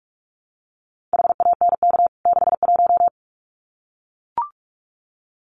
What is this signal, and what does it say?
Morse code, heard as a keyed tone, about 44 wpm, 700 Hz. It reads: HANK 61
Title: CWT Exchange 45wpm